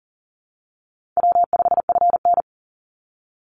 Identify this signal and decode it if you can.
Morse code, heard as a keyed tone, about 40 wpm, 700 Hz. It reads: W5FN
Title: Easy Call Signs Repeated 3x - 40wpm